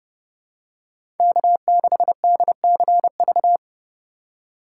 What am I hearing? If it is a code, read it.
Morse code, heard as a keyed tone, about 30 wpm, 700 Hz. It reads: K6DCV
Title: USA Call Signs - 5 Characters - 4x Character Spacing - 30wpm